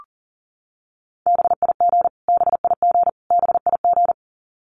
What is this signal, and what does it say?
Morse code, heard as a keyed tone, about 40 wpm, 700 Hz. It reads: BIG BIG BIG
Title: Top 500 Words Repeated 40wpm